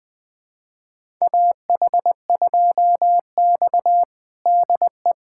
Morse code, heard as a keyed tone, about 20 wpm, 700 Hz. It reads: AH2X DE
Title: USA Callsign DE Callsign - 4 Characters - 20wpm